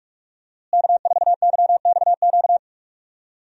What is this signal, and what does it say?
Morse code, heard as a keyed tone, about 44 wpm, 700 Hz. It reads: K4YXQ
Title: USA Call Signs - 5 Characters - Repeated 3x - 45wpm